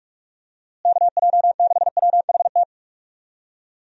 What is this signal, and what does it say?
Morse code, heard as a keyed tone, about 46 wpm, 700 Hz. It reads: KJ6WHT
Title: USA Call Signs - 6 Characters - Repeated 3x - 45wpm